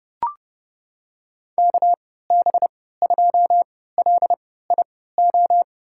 Morse code, heard as a keyed tone, about 30 wpm, spaced as for 17 wpm, 700 Hz. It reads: KB2LIO
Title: USA Call Signs - 6 Characters - 3x Character Spacing - 30wpm